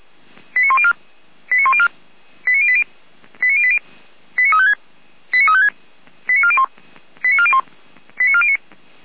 Five-tone selective call (ZVEI), heard as quick runs of five tones, tones of about 70 ms. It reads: 89104 89104 89080 89080 89356 89356 89401 89401 89409